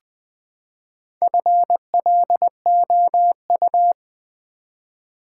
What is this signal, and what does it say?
Morse code, heard as a keyed tone, about 20 wpm, 700 Hz. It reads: FLOU
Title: Build A Word - 4 Letters At A Time - 8-Letter Words - 20wpm